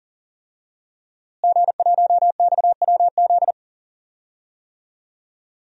Morse code, heard as a keyed tone, about 40 wpm, 700 Hz. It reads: G1XWZ
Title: DXCC-100 Call Signs - 5 Characters - Repeated 3x - 40wpm